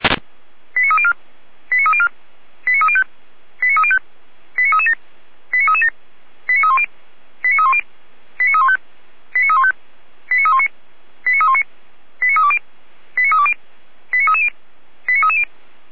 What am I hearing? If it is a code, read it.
Five-tone selective call (ZVEI), heard as quick runs of five tones, tones of about 70 ms. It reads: 89304 89304 89305 89305 89337 89337 89310 89310 89315 89315 89319 89319 89320 89320 89330 89330